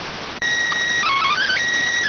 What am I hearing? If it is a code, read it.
SSTV (slow-scan television), PD 290.